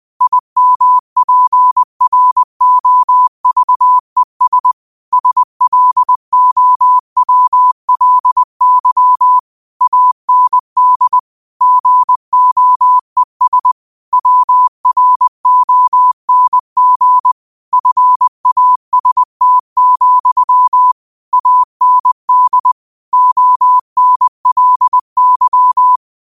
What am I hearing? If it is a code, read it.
Morse code, heard as a keyed tone, about 20 wpm, 1000 Hz. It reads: IMPROVES SLOWLY AND GOES WRONG FAST, AND ONLY